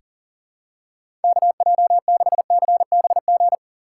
Morse code, heard as a keyed tone, about 37 wpm, 700 Hz. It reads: KJ6CBG
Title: USA Call Signs - 6 Characters - 6x Character Spacing - 40wpm